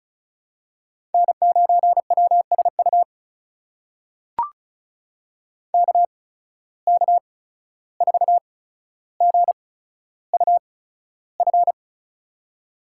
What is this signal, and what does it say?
Morse code, heard as a keyed tone, about 36 wpm, spaced as for 14 wpm, 700 Hz. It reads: N9WSU KK4GUF
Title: USA Call Signs - 8x Character Spacing - 35wpm